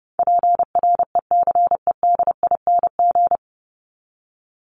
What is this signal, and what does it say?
Morse code, heard as a keyed tone, about 30 wpm, 700 Hz. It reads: PRECEDING
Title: ING Suffix - Top 100 Words with Base Words 30wpm